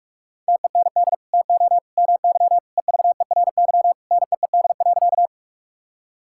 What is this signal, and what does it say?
Morse code, heard as a keyed tone, about 45 wpm, 690 Hz. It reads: TEND TO MY EVERY NEED.
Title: Sentences from Top 1000 Words - Review Part 5 - 45wpm